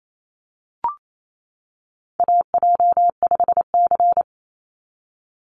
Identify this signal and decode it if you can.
Morse code, heard as a keyed tone, about 28 wpm, 700 Hz. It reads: AJ5C
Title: NA SOTA Chaser Callsigns 28wpm